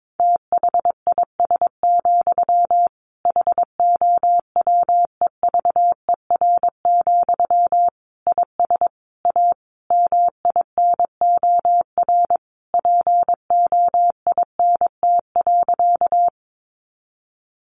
Morse code, heard as a keyed tone, about 22 wpm, 700 Hz. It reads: THIS, HOWEVER, IS A MINOR POINT.
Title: Sentences from Top 1500 Words 22wpm